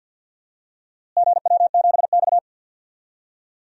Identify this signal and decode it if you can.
Morse code, heard as a keyed tone, about 49 wpm, 700 Hz. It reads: GW7X
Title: DXCC-100 Call Signs - 4 Characters - Repeated 3x - 50wpm